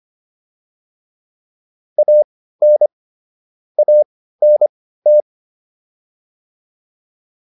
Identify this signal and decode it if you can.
Morse code, heard as a keyed tone, about 25 wpm, spaced as for 15 wpm, 600 Hz. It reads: AN ANT